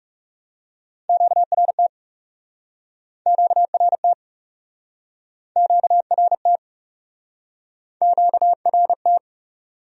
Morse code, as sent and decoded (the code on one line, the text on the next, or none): --.- .-. - / --.- .-. - / --.- .-. - / --.- .-. -
QRT QRT QRT QRT